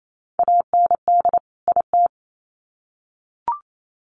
Morse code, heard as a keyed tone, about 28 wpm, 700 Hz. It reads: AND IT
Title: Sets of 2 Words 28wpm